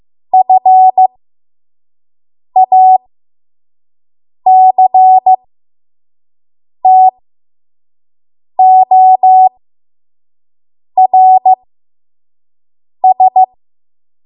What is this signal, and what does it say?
Morse code, heard as a keyed tone, about 15 wpm, spaced as for 5 wpm, 750 Hz. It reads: FACTORS
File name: Morse / 05WPM